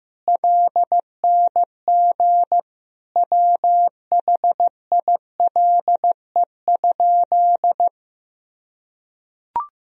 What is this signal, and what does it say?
Morse code, heard as a keyed tone, about 16 wpm, 700 Hz. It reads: LNG WHILE?
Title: Sentences from Top 300 Words and Abbreviations 15wpm